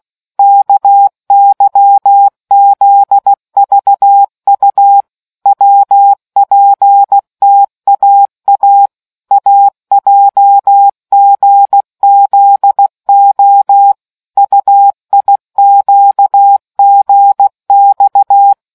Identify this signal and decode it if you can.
Morse code, heard as a keyed tone, about 16 wpm, 800 Hz. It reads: KYZVU WPTAA AJGZO UIQGX